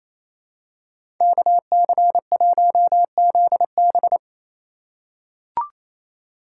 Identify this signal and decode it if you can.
Morse code, heard as a keyed tone, about 28 wpm, 700 Hz. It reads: KC1ZB
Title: USA Call Signs - 2x Character Spacing - 28wpm